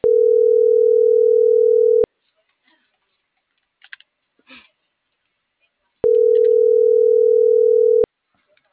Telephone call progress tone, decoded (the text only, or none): ringback tone